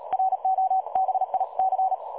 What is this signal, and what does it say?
Morse code, heard as a keyed tone, about 37 wpm, 760 Hz. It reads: ROBIC
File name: dlazaawansowanych / nagraniazpasma / websdr recording 3565.0kHz